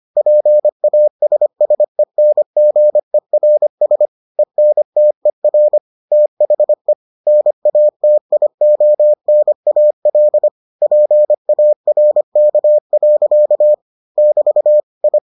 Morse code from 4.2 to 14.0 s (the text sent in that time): ENTER THE NATIONAL PARK.